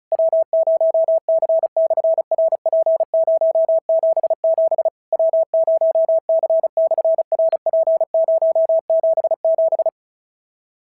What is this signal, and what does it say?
Morse code, heard as a keyed tone, about 36 wpm, 650 Hz. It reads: W0C/RP077 W0C/RP077
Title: SOTA QSO Practice 35wpm